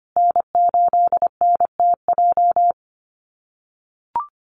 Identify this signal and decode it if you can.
Morse code, heard as a keyed tone, about 25 wpm, 700 Hz. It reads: N8NTJ
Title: USA Call Signs Repeated 3x 25wpm